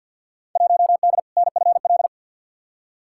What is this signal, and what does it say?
Morse code, heard as a keyed tone, about 53 wpm, 700 Hz. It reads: 1D NFL
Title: Field Day Exchanges - 50wpm